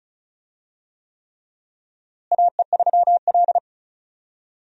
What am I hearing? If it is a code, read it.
Morse code, heard as a keyed tone, about 34 wpm, 700 Hz. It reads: AE3L
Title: USA Call Signs - 4 Characters - Repeated 3x - 35wpm